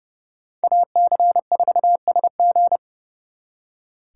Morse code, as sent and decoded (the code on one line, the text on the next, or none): .- -.-. ....- ... --.
AC4SG